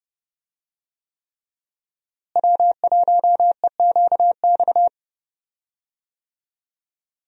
Morse code, heard as a keyed tone, about 30 wpm, 700 Hz. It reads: W1EQX
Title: USA Call Signs - 5 Characters - Repeated 3x - 30wpm